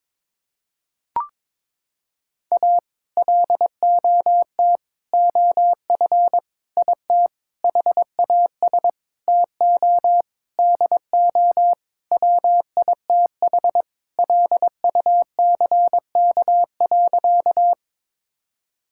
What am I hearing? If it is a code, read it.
Morse code, heard as a keyed tone, about 22 wpm, 700 Hz. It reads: A LOT OF IT HAS TO DO WITH LUCK.
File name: Sentences with 7-Word Phrases - 22wpm